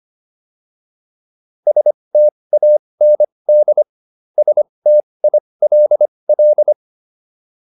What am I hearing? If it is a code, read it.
Morse code, heard as a keyed tone, about 25 wpm, spaced as for 20 wpm, 600 Hz. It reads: STAND STILL